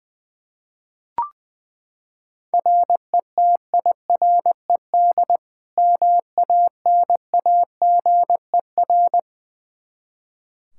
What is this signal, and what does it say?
Morse code, heard as a keyed tone, about 20 wpm, 700 Hz. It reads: RETIRED MANAGER
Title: Practice QSO - Exchange 3 - More Info - 20wpm